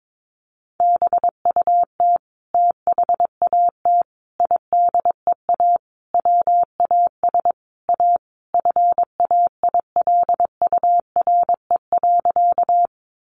Morse code, heard as a keyed tone, about 22 wpm, 700 Hz. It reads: BUT THAT IDEA WAS A FAILURE.
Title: Sentences from Top 900 Words 22wpm